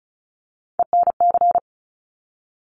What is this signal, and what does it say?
Morse code, heard as a keyed tone, about 36 wpm, 700 Hz. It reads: ENC
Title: Top Three Letter Combinations in Words 35wpm